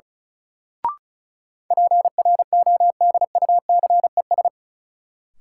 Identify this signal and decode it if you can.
Morse code, heard as a keyed tone, about 34 wpm, 700 Hz. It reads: PRODUCES
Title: Sentences from Top 1000 Words - Review Part 6 - New Word Drill - 35wpm